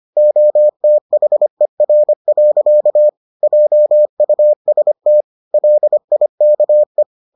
Morse code, heard as a keyed tone, about 25 wpm, 600 Hz. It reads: OTHER. JUST LIKE